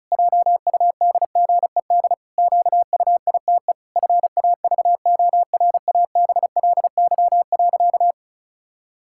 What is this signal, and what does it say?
Morse code, heard as a keyed tone, about 34 wpm, 700 Hz. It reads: JUDGED QUITE FAVORABLY.
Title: Pangram Challenge v2 - 35wpm